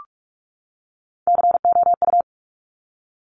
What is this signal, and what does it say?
Morse code, heard as a keyed tone, about 46 wpm, 690 Hz. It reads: COU
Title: Top Three Letter Combinations in Words 45wpm